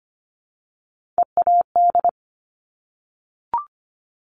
Morse code, heard as a keyed tone, about 25 wpm, 700 Hz. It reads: EAD
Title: Top Three Letter Combinations in Words 25wpm